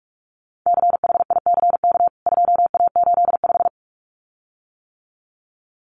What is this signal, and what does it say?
Morse code, heard as a keyed tone, about 44 wpm, 700 Hz. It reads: CHICK 2A85
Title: CWT Exchange 45wpm